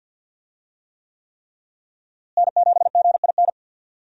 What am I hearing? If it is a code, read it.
Morse code, heard as a keyed tone, about 53 wpm, 700 Hz. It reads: N7GIN
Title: USA Call Signs - 5 Characters - Repeated 3x - 50wpm